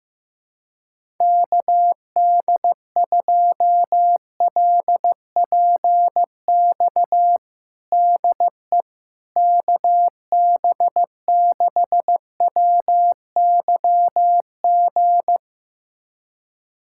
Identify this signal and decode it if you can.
Morse code, heard as a keyed tone, about 16 wpm, 700 Hz. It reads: KD2LPX DE KB6WYG
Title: USA Callsign DE Callsign - 6 Characters - 15wpm